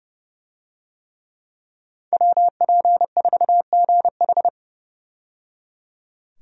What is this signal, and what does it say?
Morse code, heard as a keyed tone, about 29 wpm, 700 Hz. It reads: WP4GH
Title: USA Call Signs - 5 Characters - 30wpm